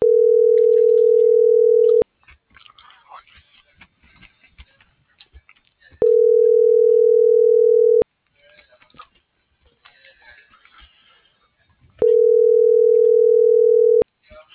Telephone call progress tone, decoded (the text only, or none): ringback tone